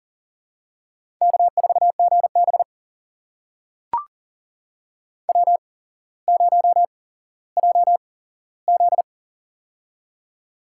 Morse code, heard as a keyed tone, about 37 wpm, spaced as for 17 wpm, 700 Hz. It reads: K4GB W0JZ
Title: USA Call Signs - 4 Characters - 8x Character Spacing - 40wpm